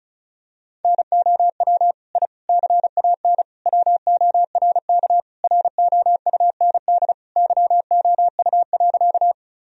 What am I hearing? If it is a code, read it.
Morse code, heard as a keyed tone, about 34 wpm, 700 Hz. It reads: NOW I CAN WORK ROUND YOU.
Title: Sentences from Top 1000 Words - Review Part 1 - 35wpm